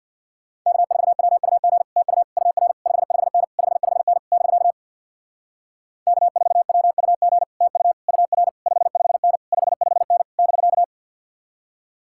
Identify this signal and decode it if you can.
Morse code, heard as a keyed tone, about 62 wpm, 700 Hz. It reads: K4WUG TU UR 55N 55N <BK> K4WUG TU UR 55N 55N <BK>
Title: Speed-Racing Common QSO Phrases Encore - 50wpm